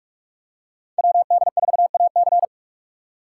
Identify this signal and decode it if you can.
Morse code, heard as a keyed tone, about 44 wpm, 700 Hz. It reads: WD4AC